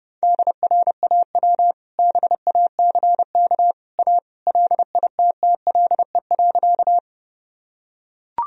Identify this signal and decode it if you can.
Morse code, heard as a keyed tone, about 30 wpm, 700 Hz. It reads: DRAW BACK A LITTLE.
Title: Sentences from Top 1000 Words - Review Part 1 - 30wpm